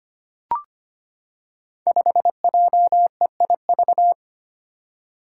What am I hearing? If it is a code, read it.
Morse code, heard as a keyed tone, about 25 wpm, 700 Hz. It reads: 5JEIV
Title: Five Challenging Letter-Number Combinations 25wpm